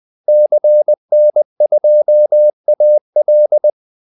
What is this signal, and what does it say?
Morse code, heard as a keyed tone, about 20 wpm, 600 Hz. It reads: CN2AL
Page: Kaldesignaler 20 wpm